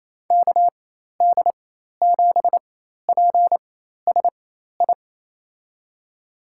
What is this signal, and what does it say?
Morse code, heard as a keyed tone, about 27 wpm, spaced as for 13 wpm, 700 Hz. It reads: KD7PSI
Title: USA Call Signs - 4x Character Spacing - 28wpm